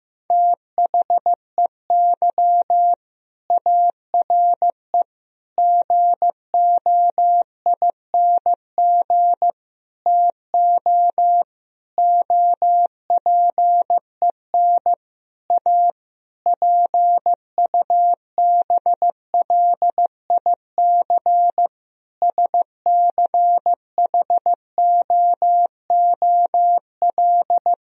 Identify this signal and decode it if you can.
Morse code, heard as a keyed tone, about 16 wpm, 700 Hz. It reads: THEY ARE GOING TO OPEN A PUBLIC SCHOOL
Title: Sentences from Top 1000 Words - Review Part 3 - 15wpm